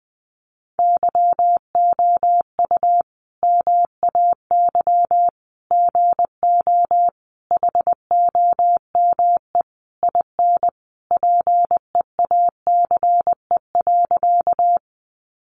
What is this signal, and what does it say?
Morse code, heard as a keyed tone, about 20 wpm, 700 Hz. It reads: YOU MAY GO HOME IN PEACE.
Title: Sentences from Top 1500 Words 20wpm